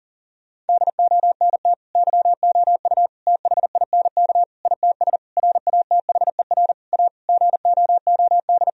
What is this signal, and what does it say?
Morse code, heard as a keyed tone, about 37 wpm, 700 Hz. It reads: DONT YOU THINK ITS RATHER A GOOD
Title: Sentences from Top 1000 Words - Review Part 4 - 40wpm